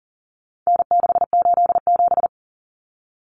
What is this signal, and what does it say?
Morse code, heard as a keyed tone, about 40 wpm, 700 Hz. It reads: N687
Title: Common Cut Numbers - 4-Digit 40wpm